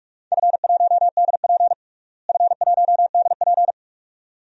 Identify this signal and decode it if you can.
Morse code, heard as a keyed tone, about 43 wpm, 700 Hz. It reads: F1DP F1DP
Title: DXCC-100 Call Signs - 4 Characters - Repeated 2x - 45wpm